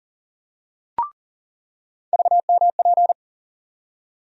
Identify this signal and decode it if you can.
Morse code, heard as a keyed tone, about 40 wpm, 700 Hz. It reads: VMP
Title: USA Call Sign Postfixes - 3 Letters 40wpm